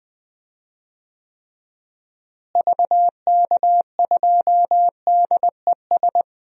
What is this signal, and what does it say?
Morse code, heard as a keyed tone, about 20 wpm, 700 Hz. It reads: VK2DES
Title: DXCC-100 Call Signs - 6 Characters - Repeated 2x - 20wpm